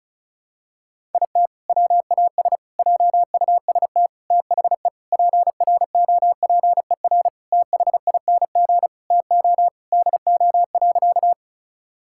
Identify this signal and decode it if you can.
Morse code, heard as a keyed tone, about 36 wpm, 700 Hz. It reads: IT WAS JUST THE PROPER THING TO DO.
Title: Sentences from Top 1000 Words - Review Part 5 - 35wpm